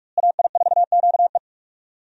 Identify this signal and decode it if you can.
Morse code, heard as a keyed tone, about 43 wpm, 700 Hz. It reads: AI4QE
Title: USA Call Signs - 5 Characters - Repeated 3x - 45wpm